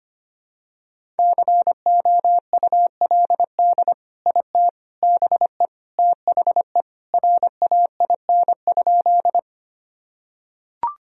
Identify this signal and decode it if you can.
Morse code, heard as a keyed tone, about 25 wpm, 700 Hz. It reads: COULD IT BE THE RAIN?
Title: Sentences from Top 1000 Words - Review Part 1 - 25wpm